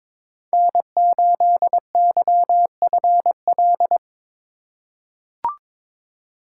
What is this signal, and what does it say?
Morse code, heard as a keyed tone, about 22 wpm, 700 Hz. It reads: N8YFL
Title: USA Call Signs - 2x Character Spacing - 22wpm